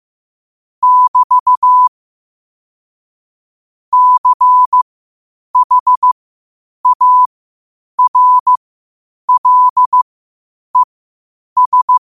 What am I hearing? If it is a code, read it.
Morse code, heard as a keyed tone, about 15 wpm, spaced as for 8 wpm, 1000 Hz. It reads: <BT> CHARLES
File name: qotdfiles / 8wpm